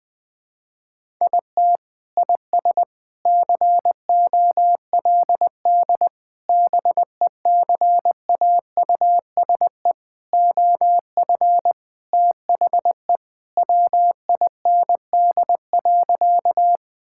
Morse code, heard as a keyed tone, about 21 wpm, 700 Hz. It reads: IT IS COLD BECAUSE OF THE WIND.
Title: Sentences with 3-Word Phrases - Spoken Beginning - 20wpm